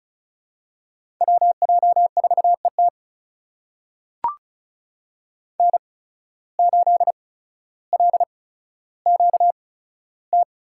Morse code, heard as a keyed tone, about 36 wpm, spaced as for 15 wpm, 700 Hz. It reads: WJ4ET N8LQT
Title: USA Call Signs - 5 Characters - 8x Character Spacing - 35wpm